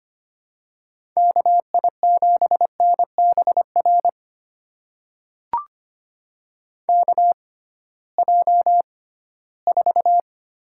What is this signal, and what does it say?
Morse code, heard as a keyed tone, about 25 wpm, spaced as for 16 wpm, 700 Hz. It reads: KI7NBR KJ4
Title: USA Call Signs - 6x Character Spacing - 25wpm